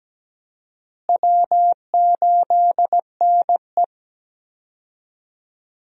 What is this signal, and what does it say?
Morse code, heard as a keyed tone, about 17 wpm, 700 Hz. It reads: W8NE
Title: USA Call Signs - 4 Characters - Repeated 3x - 17wpm